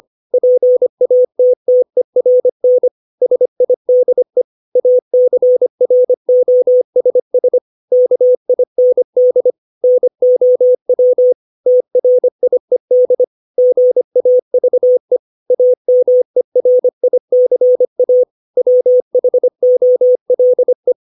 Morse code, heard as a keyed tone, about 25 wpm, 500 Hz. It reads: PATTERN SIDE ACROSS KIND NOW TRIED GAVE AMERICA WHOLE
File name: 632 words / 632 Words 25WPM